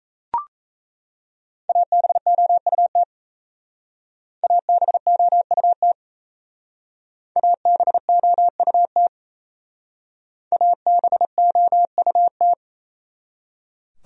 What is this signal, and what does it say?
Morse code, heard as a keyed tone, about 35 wpm, 700 Hz. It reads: ABOUT ABOUT ABOUT ABOUT